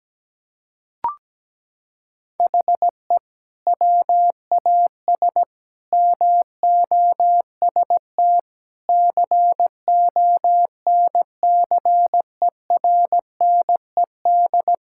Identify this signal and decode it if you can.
Morse code, heard as a keyed tone, about 17 wpm, 700 Hz. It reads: HE WAS MOST CONCERNED
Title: Sentences from Top 1000 Words - Review Part 6 - 17wpm